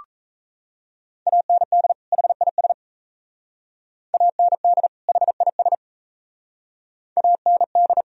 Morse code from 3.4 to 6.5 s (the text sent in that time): AND HIS